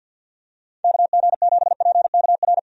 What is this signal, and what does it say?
Morse code, heard as a keyed tone, about 53 wpm, 700 Hz. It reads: KG7PKR
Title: USA Call Signs - 6 Characters - 50wpm